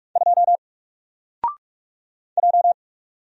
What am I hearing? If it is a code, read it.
Morse code, heard as a keyed tone, about 44 wpm, 700 Hz. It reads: 2 J
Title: Single Letter-Number 45wpm